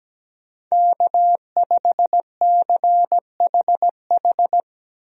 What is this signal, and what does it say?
Morse code, heard as a keyed tone, about 17 wpm, 700 Hz. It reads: K5CHH
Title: USA Call Signs - 5 Characters - Repeated 3x - 17wpm